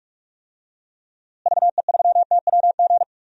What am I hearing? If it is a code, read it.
Morse code, heard as a keyed tone, about 47 wpm, 700 Hz. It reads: VE3TWG
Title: DXCC-100 Call Signs - 6 Characters - Repeated 3x - 45wpm